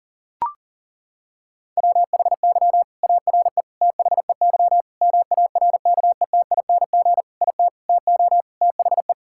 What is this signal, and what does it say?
Morse code, heard as a keyed tone, about 40 wpm, 700 Hz. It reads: WHY ARE THEY MARKETING IT TO THE
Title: Sentences from Top 1000 Words - Review Part 5 - 40wpm